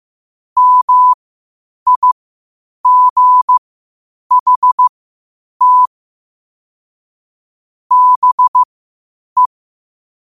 Morse code, heard as a keyed tone, about 15 wpm, spaced as for 8 wpm, 1000 Hz. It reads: MIGHT BE
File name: qotdfiles / 8wpm